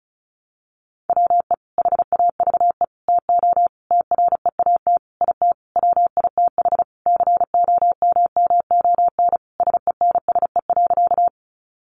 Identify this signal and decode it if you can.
Morse code, heard as a keyed tone, about 34 wpm, 700 Hz. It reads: WE HAVE TO TREAT IT WITH COMMON SENSE.
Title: Sentences from Top 1300 Words 35wpm